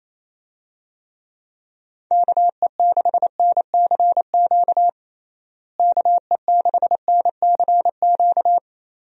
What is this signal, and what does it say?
Morse code, heard as a keyed tone, about 28 wpm, 700 Hz. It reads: KE6NCQ KE6NCQ
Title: USA Call Signs - 6 Characters - Repeated 2x - 28wpm